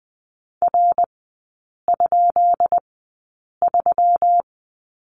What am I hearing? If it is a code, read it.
Morse code, heard as a keyed tone, about 20 wpm, spaced as for 8 wpm, 690 Hz. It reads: R?3
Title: 212 Lesson 34e - Number 8 - Three Characters 20wpm